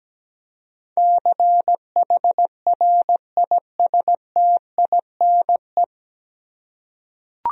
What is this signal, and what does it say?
Morse code, heard as a keyed tone, about 17 wpm, 700 Hz. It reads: CHRISTINE
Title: USA Common Names - 17wpm